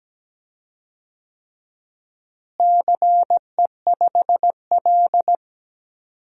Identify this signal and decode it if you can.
Morse code, heard as a keyed tone, about 17 wpm, 700 Hz. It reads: CE5L
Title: DXCC-100 Call Signs - 4 Characters - Repeated 3x - 17wpm